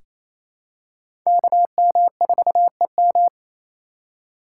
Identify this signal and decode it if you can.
Morse code, heard as a keyed tone, about 28 wpm, 700 Hz. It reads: KM4EM
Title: USA Call Signs - 5 Characters - 6x Character Spacing - 28wpm